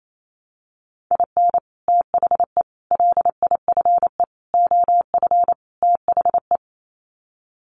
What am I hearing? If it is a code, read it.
Morse code, heard as a keyed tone, about 28 wpm, 700 Hz. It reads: IN THE LIFE OF THE
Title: Sets of 5 Words Encore2 28wpm